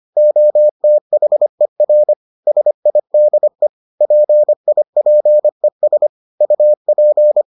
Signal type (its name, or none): Morse code